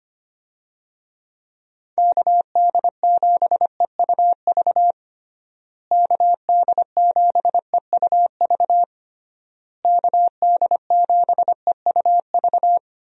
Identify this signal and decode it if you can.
Morse code, heard as a keyed tone, about 25 wpm, 700 Hz. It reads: KD7EUV KD7EUV KD7EUV
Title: USA Call Signs - 6 Characters - Repeated 3x - 25wpm